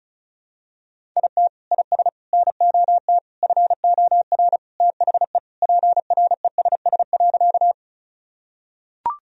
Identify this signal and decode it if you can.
Morse code, heard as a keyed tone, about 36 wpm, 700 Hz. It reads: IT IS NOT FOR THE PRESS.
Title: Sentences from Top 1000 Words - Review Part 1 - 35wpm